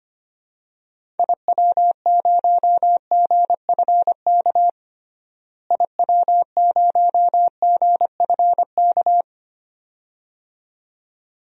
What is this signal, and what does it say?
Morse code, heard as a keyed tone, about 25 wpm, 700 Hz. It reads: IW0GFK IW0GFK
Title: DXCC-100 Call Signs - 6 Characters - Repeated 2x - 25wpm